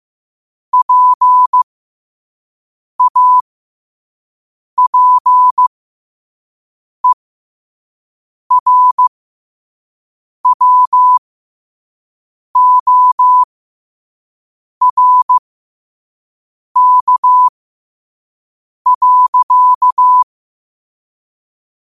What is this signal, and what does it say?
Morse code, heard as a keyed tone, about 15 wpm, spaced as for 5 wpm, 1000 Hz. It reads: PAPERWORK.
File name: qotdfiles / 5wpm